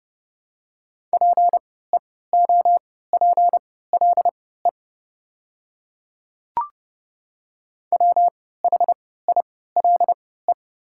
Morse code, heard as a keyed tone, about 30 wpm, spaced as for 17 wpm, 690 Hz. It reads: PEOPLE WHILE
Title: 3x Character Spacing - Top 300 Words - 30wpm